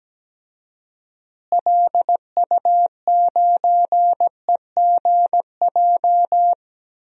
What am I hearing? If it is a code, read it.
Morse code, heard as a keyed tone, about 17 wpm, 700 Hz. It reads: LU9EGJ
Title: DXCC-100 Call Signs - 6 Characters - Repeated 2x - 17wpm